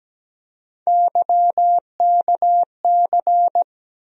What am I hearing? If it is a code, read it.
Morse code, heard as a keyed tone, about 17 wpm, 700 Hz. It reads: YKC
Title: USA Call Sign Postfixes - 3 Letters 17wpm